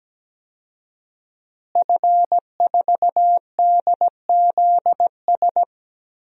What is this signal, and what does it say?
Morse code, heard as a keyed tone, about 17 wpm, 700 Hz. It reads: F4DZS
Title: DXCC-100 Call Signs - 5 Characters - Repeated 3x - 17wpm